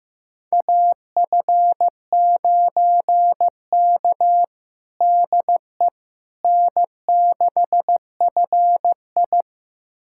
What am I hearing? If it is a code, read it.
Morse code, heard as a keyed tone, about 16 wpm, 700 Hz. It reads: AF9K DE N6FI
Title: USA Callsign DE Callsign - 4 Characters - 15wpm